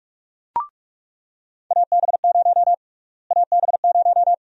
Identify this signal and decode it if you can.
Morse code, heard as a keyed tone, about 47 wpm, 700 Hz. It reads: AB0 AB0
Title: USA Call Sign Prefixes - Repeated 2x - 45wpm